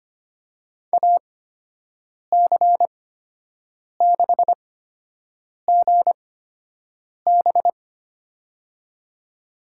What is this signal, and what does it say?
Morse code, heard as a keyed tone, about 25 wpm, spaced as for 7 wpm, 700 Hz. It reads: AC6GB
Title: USA Call Signs - 5 Characters - 8x Character Spacing - 25wpm